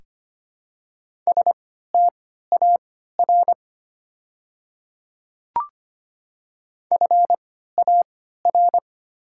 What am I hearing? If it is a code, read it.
Morse code, heard as a keyed tone, about 25 wpm, spaced as for 14 wpm, 700 Hz. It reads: STAR FAR
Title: 3x Character Spacing - Top 500 Words - 25wpm